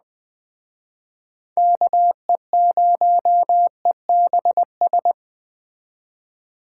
Morse code, heard as a keyed tone, about 20 wpm, 700 Hz. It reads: KE0EBS
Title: Speed Racing - USA 2x3 Call Signs 17wpm